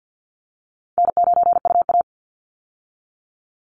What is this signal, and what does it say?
Morse code, heard as a keyed tone, about 50 wpm, 700 Hz. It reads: N9UA
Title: Easy Call Signs - 50wpm